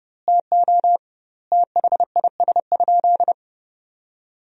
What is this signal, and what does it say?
Morse code, heard as a keyed tone, about 30 wpm, 700 Hz. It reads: TO THIS?
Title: 2x Word-Spacing - Sentences from Top 100 Words - 30wpm